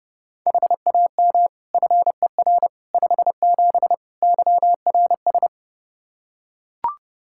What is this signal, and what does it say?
Morse code, heard as a keyed tone, about 30 wpm, 700 Hz. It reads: HAM FER 57 YRS
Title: K1USN SST QSO Practice 30wpm